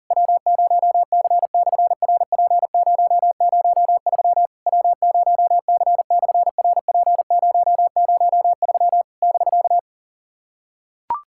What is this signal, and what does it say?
Morse code, heard as a keyed tone, about 39 wpm, 700 Hz. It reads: W0C/RP003 W0C/RP003 <BK>
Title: SOTA QSO Practice 40wpm